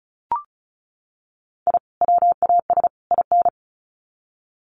Morse code, heard as a keyed tone, about 34 wpm, 700 Hz. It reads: I WAS IN
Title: Sets of 3 Words 35wpm